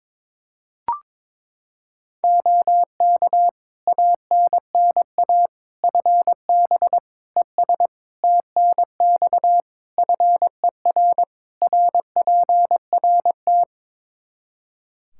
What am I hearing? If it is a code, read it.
Morse code, heard as a keyed tone, about 22 wpm, 700 Hz. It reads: OK ANNA FB ES TNX FER RPRT
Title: Practice QSO - Exchange 2 - Info - 22wpm